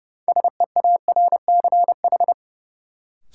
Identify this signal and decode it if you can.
Morse code, heard as a keyed tone, about 30 wpm, 700 Hz. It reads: SEARCH
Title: Sentences from Top 800 Words - New Word Drill - 30wpm